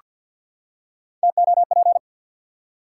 Morse code, heard as a keyed tone, about 48 wpm, 700 Hz. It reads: TOP
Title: Three Letter Words - 50wpm